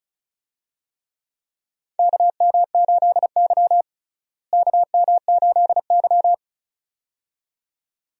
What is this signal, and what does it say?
Morse code, heard as a keyed tone, about 34 wpm, 700 Hz. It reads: KM8Y KM8Y
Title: USA Call Signs - 4 Characters - Repeated 2x - 35wpm